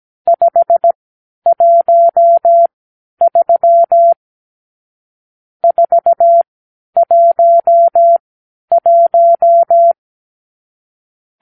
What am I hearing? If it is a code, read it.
Morse code, heard as a keyed tone, about 17 wpm, spaced as for 10 wpm, 680 Hz. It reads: 513 411
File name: arrl / Review (1-5)